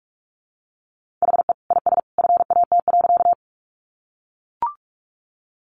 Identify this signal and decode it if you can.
Morse code, heard as a keyed tone, about 44 wpm, 700 Hz. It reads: HE IS FAT.
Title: Sentences from Top 600 Words 45wpm